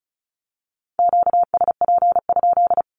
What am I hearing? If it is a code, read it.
Morse code, heard as a keyed tone, about 36 wpm, 700 Hz. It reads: QSP?
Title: Q-Codes with Spoken Meaning - 35wpm